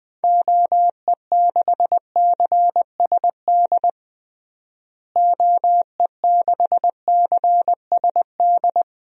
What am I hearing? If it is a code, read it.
Morse code, heard as a keyed tone, about 20 wpm, 700 Hz. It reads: OE6CSD OE6CSD
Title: DXCC-100 Call Signs - 6 Characters - Repeated 2x - 20wpm